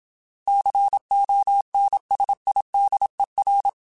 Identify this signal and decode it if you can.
Morse code, heard as a keyed tone, about 27 wpm, 800 Hz. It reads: CONSIDER
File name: Technical Subjects / Technical Subjects-mp3s / AND Gate @25